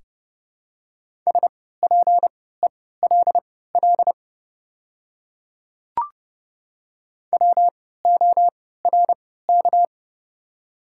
Morse code, heard as a keyed tone, about 30 wpm, spaced as for 17 wpm, 700 Hz. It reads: SPELL WORK
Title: 3x Character Spacing - Top 200 Words - 30wpm